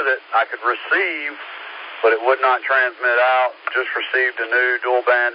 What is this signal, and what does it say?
Emergency Alert System (SAME) tones, heard as harsh buzzing bursts, with speech. It is heard from 0.8 to 4.8 s.